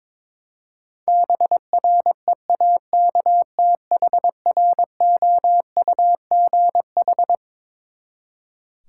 Morse code, heard as a keyed tone, about 22 wpm, 700 Hz. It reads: BREAKTHROUGH
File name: Twelve Letter Words - 22wpm